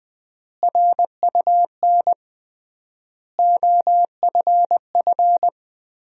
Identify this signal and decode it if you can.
Morse code, heard as a keyed tone, about 20 wpm, 700 Hz. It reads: RUN OFF
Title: 3x Word-Spacing - Sentences from Top 300 Words - 20wpm